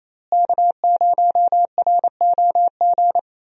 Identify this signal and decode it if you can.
Morse code, heard as a keyed tone, about 28 wpm, 700 Hz. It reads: K0ROG
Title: USA Call Signs - 5 Characters - Repeated 3x - 28wpm